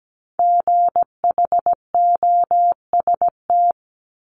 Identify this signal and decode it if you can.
Morse code, heard as a keyed tone, about 17 wpm, 700 Hz. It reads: GHOST
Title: Multi-letter Phonemes 17wpm